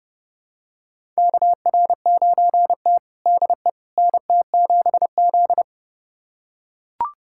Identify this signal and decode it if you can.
Morse code, heard as a keyed tone, about 30 wpm, 700 Hz. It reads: KR9T DE NT7Z
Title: USA Callsign DE Callsign - 4 Characters - 30wpm